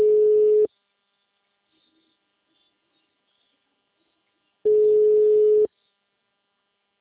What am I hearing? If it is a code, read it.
Ringback tone (Europe, 425 Hz).
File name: out- -2028- - - .127